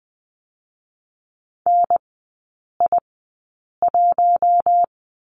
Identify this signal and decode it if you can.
Morse code, heard as a keyed tone, about 20 wpm, 700 Hz. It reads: N I 1
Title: 121 Lesson 20e - Number 3 - Three Characters 20wpm